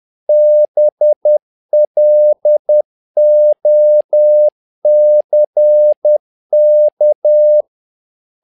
Morse code, heard as a keyed tone, about 10 wpm, 600 Hz. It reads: BLOCK
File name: learning center / tower trouble / tower trouble 10wpm